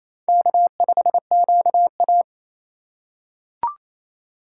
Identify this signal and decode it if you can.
Morse code, heard as a keyed tone, about 27 wpm, 700 Hz. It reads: K5QA
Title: USA Call Signs - 4 Characters - 3x Character Spacing - 28wpm